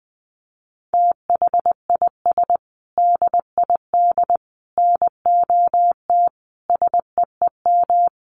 Morse code, heard as a keyed tone, about 20 wpm, 700 Hz. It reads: THIS DID NOT SEEM
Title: Sentences from Top 1400 Words 20wpm